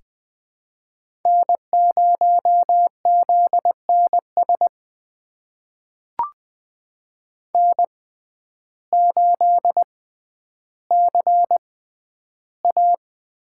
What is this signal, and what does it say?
Morse code, heard as a keyed tone, about 20 wpm, spaced as for 11 wpm, 700 Hz. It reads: N0ZNS N8CA
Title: USA Call Signs - 5 Characters - 6x Character Spacing - 20wpm